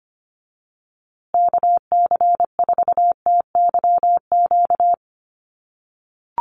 Morse code, heard as a keyed tone, about 25 wpm, 700 Hz. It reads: KC4TYQ
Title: USA Call Signs Repeated 3x 25wpm